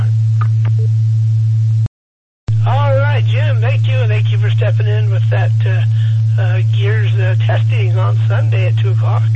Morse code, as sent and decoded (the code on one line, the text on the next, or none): none